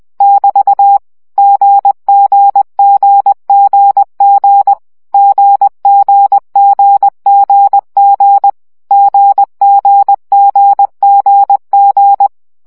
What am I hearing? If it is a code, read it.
Morse code, heard as a keyed tone, about 20 wpm, 800 Hz. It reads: <BT> GGGGG GGGGG GGGGG
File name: CW gyakorlas g 17 baud